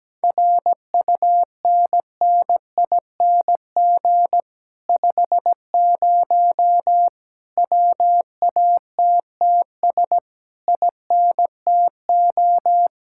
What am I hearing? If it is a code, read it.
Morse code, heard as a keyed tone, about 17 wpm, 700 Hz. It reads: RUNNING 50 WATTS INTO
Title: Speed-Racing Common QSO Phrases - 17wpm